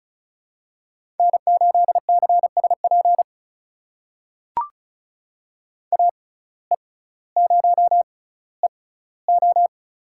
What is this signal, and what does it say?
Morse code, heard as a keyed tone, about 36 wpm, spaced as for 18 wpm, 700 Hz. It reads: N8CSP AE0EO
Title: USA Call Signs - 5 Characters - 6x Character Spacing - 35wpm